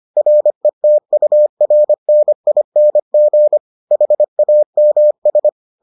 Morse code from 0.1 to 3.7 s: RETURNING

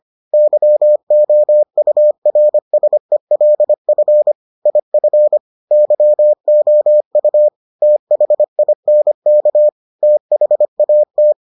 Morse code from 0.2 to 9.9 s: YOURSELF IF YOU THINK

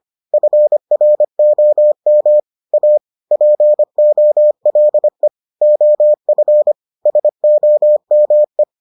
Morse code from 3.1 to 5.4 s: POLE

